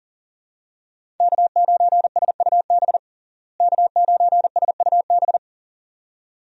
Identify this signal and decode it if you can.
Morse code, heard as a keyed tone, about 40 wpm, 700 Hz. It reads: K9SUB K9SUB
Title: USA Call Signs - 5 Characters - Repeated 2x - 40wpm